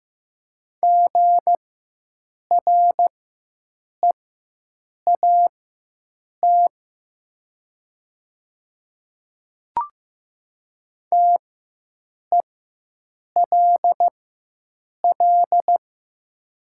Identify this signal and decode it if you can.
Morse code, heard as a keyed tone, about 15 wpm, spaced as for 7 wpm, 700 Hz. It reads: GREAT TELL
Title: 4x Character Spacing - Top 200 Words - 15wpm